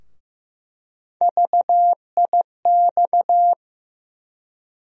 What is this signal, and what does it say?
Morse code, heard as a keyed tone, about 15 wpm, 700 Hz. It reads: VIX
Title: USA Call Sign Postfixes - 3 Letters 15wpm